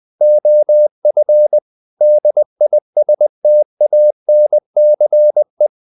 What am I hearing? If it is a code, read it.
Morse code, heard as a keyed tone, about 20 wpm, 600 Hz. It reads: OF DISTANCE
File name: CW / Practice / If20